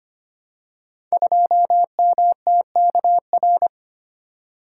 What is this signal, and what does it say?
Morse code, heard as a keyed tone, about 25 wpm, 700 Hz. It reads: 2MTKR
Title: Code Groups - 5 Letter:Numbers - 25wpm